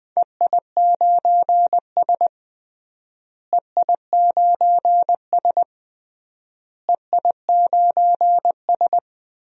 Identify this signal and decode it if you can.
Morse code, heard as a keyed tone, about 20 wpm, 700 Hz. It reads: EI9S EI9S EI9S
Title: DXCC-100 Call Signs - 4 Characters - Repeated 3x - 20wpm